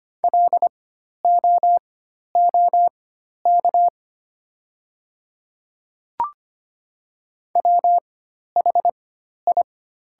Morse code, heard as a keyed tone, about 25 wpm, spaced as for 12 wpm, 700 Hz. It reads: LOOK WHI
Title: 4x Character Spacing - Top 100 Words - 25wpm